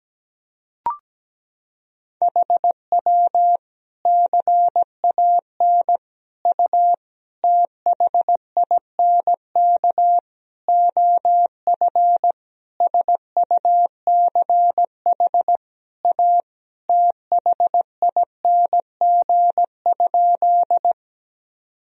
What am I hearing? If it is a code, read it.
Morse code, heard as a keyed tone, about 17 wpm, 700 Hz. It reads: HW CAN U THINK OF SUCH A THING?
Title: Sentences from Top 200 Words and Abbreviations 17wpm